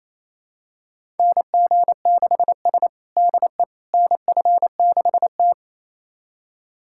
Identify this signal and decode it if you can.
Morse code, heard as a keyed tone, about 28 wpm, 700 Hz. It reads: NG6S DE NF6T
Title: USA Callsign DE Callsign - 4 Characters - 28wpm